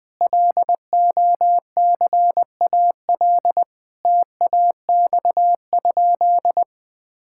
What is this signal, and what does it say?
Morse code, heard as a keyed tone, about 20 wpm, 700 Hz. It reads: LOCAL TAX?
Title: Sentences from Top 1000 Words - Review Part 5 - 20wpm